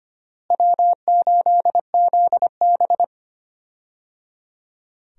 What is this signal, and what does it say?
Morse code, heard as a keyed tone, about 25 wpm, 700 Hz. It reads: W8ZB
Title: USA Call Signs - 4 Characters - Repeated 3x - 25wpm